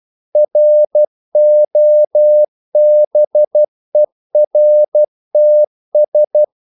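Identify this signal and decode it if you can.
Morse code, heard as a keyed tone, about 12 wpm, 600 Hz. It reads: ROBERTS
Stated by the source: Morse practice sending English quotations.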